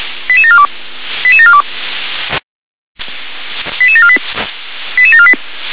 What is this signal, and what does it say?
Five-tone selective call (ZVEI), heard as quick runs of five tones, tones of about 70 ms. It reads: 88742 88742 88748 88748
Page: /357/SDR/ZVEI/88749/